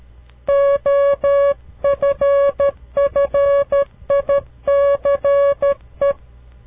Morse code, heard as a keyed tone, about 13 wpm, 560 Hz. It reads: OFFICE
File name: CW Learning Programs / farnsworth / unit 2